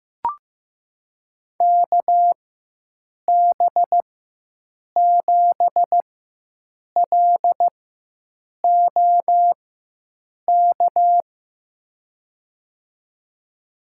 Morse code, heard as a keyed tone, about 15 wpm, spaced as for 7 wpm, 700 Hz. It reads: KB7LOK
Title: USA Call Signs - 6 Characters - 4x Character Spacing - 15wpm